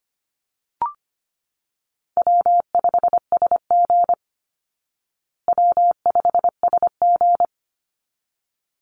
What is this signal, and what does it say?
Morse code, heard as a keyed tone, about 25 wpm, 700 Hz. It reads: W5SG W5SG
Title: Easy Call Signs Repeated 2x - 25wpm